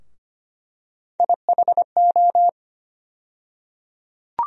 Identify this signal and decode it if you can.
Morse code, heard as a keyed tone, about 25 wpm, 700 Hz. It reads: IHO
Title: USA Call Sign Postfixes - 3 Letters 25wpm